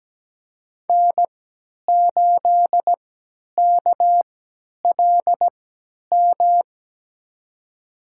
Morse code, heard as a keyed tone, about 17 wpm, spaced as for 10 wpm, 700 Hz. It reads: N8KLM
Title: USA Call Signs - 5 Characters - 3x Character Spacing - 17wpm